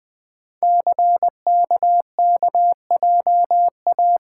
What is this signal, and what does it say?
Morse code, heard as a keyed tone, about 20 wpm, 700 Hz. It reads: CKKJA
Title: Code Groups - 5 Letters - 20wpm